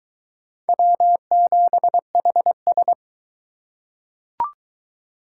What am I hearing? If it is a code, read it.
Morse code, heard as a keyed tone, about 19 wpm, 700 Hz. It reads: W7HS